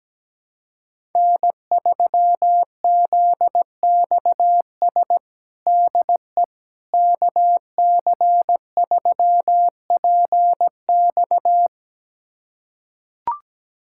Morse code, heard as a keyed tone, about 17 wpm, 700 Hz. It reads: N3ZXS DE KC3PX
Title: USA Callsign DE Callsign - 5 Characters - 17wpm